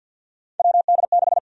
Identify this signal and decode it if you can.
Morse code, heard as a keyed tone, about 52 wpm, 700 Hz. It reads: WD6
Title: USA Call Sign Prefixes - 50wpm